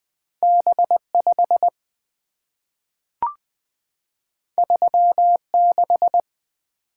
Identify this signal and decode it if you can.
Morse code, heard as a keyed tone, about 20 wpm, 700 Hz. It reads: B5 36
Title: Two Letter-Number Combinations - 20wpm